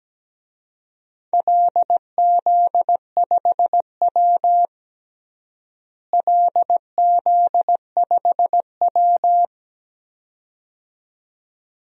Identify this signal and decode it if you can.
Morse code, heard as a keyed tone, about 17 wpm, 700 Hz. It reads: LZ5W LZ5W
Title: DXCC-100 Call Signs - 4 Characters - Repeated 2x - 17wpm